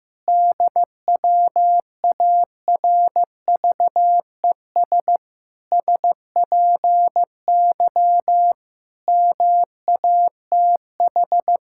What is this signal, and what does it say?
Morse code, heard as a keyed tone, about 16 wpm, 700 Hz. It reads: DWARVES SPY MATH
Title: Pangram Challenge v2 - 15wpm